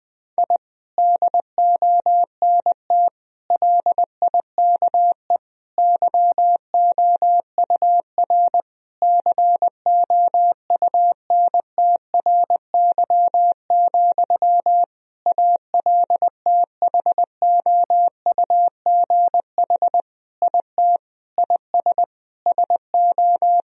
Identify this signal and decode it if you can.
Morse code, heard as a keyed tone, about 20 wpm, 700 Hz. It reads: I DONT LIKE YOUR COUNTRY, ALTHOUGH IT IS SO
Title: Sentences from Top 1000 Words - Review Part 3 - 20wpm